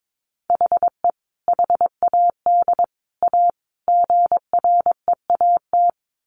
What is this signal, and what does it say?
Morse code, heard as a keyed tone, about 22 wpm, 700 Hz. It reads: HE HAD A GREAT
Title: Sentences from Top 1100 Words 22wpm